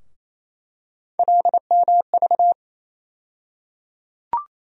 Morse code, heard as a keyed tone, about 28 wpm, 700 Hz. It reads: LMV
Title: USA Call Sign Postfixes - 3 Letters 28wpm